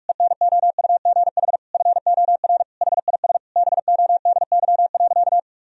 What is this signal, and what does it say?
Morse code, heard as a keyed tone, about 46 wpm, 700 Hz. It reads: ENOUGH FOR HIS BODY.
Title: Sentences from Top 1000 Words - Review Part 2 - 45wpm